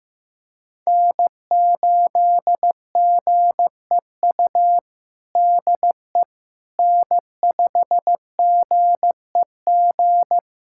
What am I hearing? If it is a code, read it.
Morse code, heard as a keyed tone, about 15 wpm, 700 Hz. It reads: N8GEU DE N5GEG
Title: USA Callsign DE Callsign - 5 Characters - 15wpm